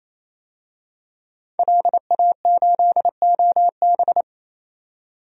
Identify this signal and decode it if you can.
Morse code, heard as a keyed tone, about 28 wpm, 700 Hz. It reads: LA8OB
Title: DXCC-100 Call Signs - 5 Characters - 28wpm